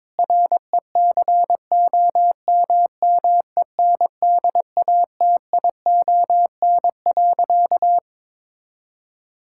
Morse code, heard as a keyed tone, about 22 wpm, 700 Hz. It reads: RECOMMENDATION.
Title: Sentences from Top 1600 Words 22wpm